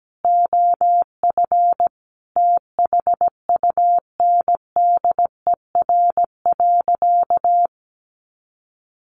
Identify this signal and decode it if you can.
Morse code, heard as a keyed tone, about 17 wpm, 700 Hz. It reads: OF THUNDER.
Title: A Sound Of Thunder 17wpm